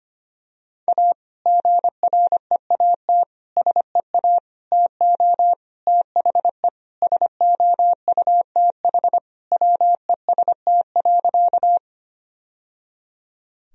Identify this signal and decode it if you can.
Morse code, heard as a keyed tone, about 25 wpm, 700 Hz. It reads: A GREAT SEA TO THE SOUTH WEST.
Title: Sentences from Top 1000 Words - Review Part 2 - 25wpm